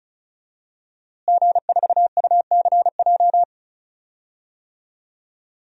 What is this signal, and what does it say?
Morse code, heard as a keyed tone, about 36 wpm, 700 Hz. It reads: G4UCJ
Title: DXCC-100 Call Signs - 5 Characters - Repeated 2x - 35wpm